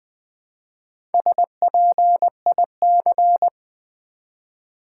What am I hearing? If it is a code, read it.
Morse code, heard as a keyed tone, about 20 wpm, 700 Hz. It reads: SPIC
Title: Build A Word - 1 Letter At A Time - 3 to 5-Letter Words - 20wpm